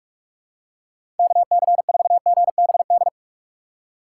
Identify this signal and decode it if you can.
Morse code, heard as a keyed tone, about 46 wpm, 700 Hz. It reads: KC4GBD
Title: USA Call Signs - 6 Characters - 2x Character Spacing - 45wpm